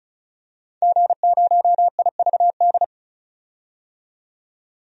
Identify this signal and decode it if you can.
Morse code, heard as a keyed tone, about 34 wpm, 700 Hz. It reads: G0IVD
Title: DXCC-100 Call Signs - 5 Characters - 35wpm